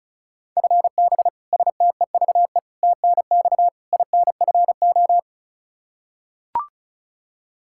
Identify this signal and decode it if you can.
Morse code, heard as a keyed tone, about 36 wpm, 700 Hz. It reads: FB STEVE TNX INFO
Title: Practice QSO - Exchange 3 - More Info - 35wpm